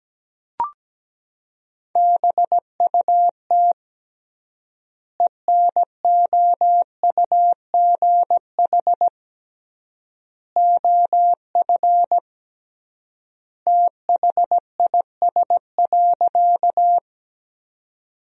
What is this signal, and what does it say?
Morse code, heard as a keyed tone, about 17 wpm, 700 Hz. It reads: BUT ENOUGH OF THIS.
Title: 3x Word-Spacing - Sentences from Top 400 Words - 17wpm